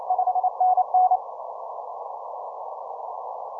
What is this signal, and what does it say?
Morse code, heard as a keyed tone, about 29 wpm, 740 Hz. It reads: 5NN
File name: simple-file-list / ZL8R 15M CW